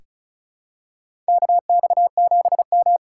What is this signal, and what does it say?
Morse code, heard as a keyed tone, about 36 wpm, 700 Hz. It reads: KX7M
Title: K1USN SST QSO Practice - 35wpm